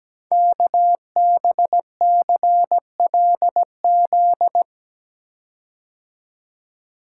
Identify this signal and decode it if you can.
Morse code, heard as a keyed tone, about 17 wpm, 700 Hz. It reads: KBCLZ
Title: Code Groups - 5 Letters - 17wpm